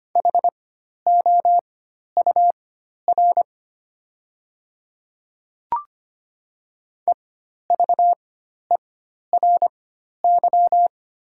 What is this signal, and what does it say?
Morse code, heard as a keyed tone, about 25 wpm, spaced as for 12 wpm, 700 Hz. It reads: HOUR EVERY
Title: 4x Character Spacing - Top 500 Words - 25wpm